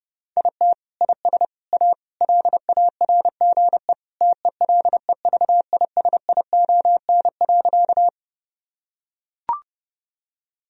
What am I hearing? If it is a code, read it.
Morse code, heard as a keyed tone, about 32 wpm, 700 Hz. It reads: IT IS A LARGE TELEVISION.
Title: Sentences from Top 1000 Words - Review Part 4 - 30wpm